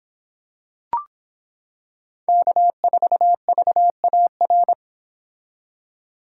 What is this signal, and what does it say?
Morse code, heard as a keyed tone, about 26 wpm, 700 Hz. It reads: K4VAR